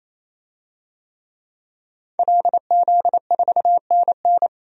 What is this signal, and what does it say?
Morse code, heard as a keyed tone, about 28 wpm, 700 Hz. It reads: LZ4NN
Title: DXCC-100 Call Signs - 5 Characters - 28wpm